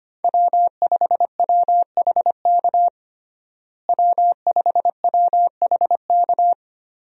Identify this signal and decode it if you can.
Morse code, heard as a keyed tone, about 25 wpm, 700 Hz. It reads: W5WHK W5WHK
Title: USA Call Signs - 5 Characters - Repeated 3x - 25wpm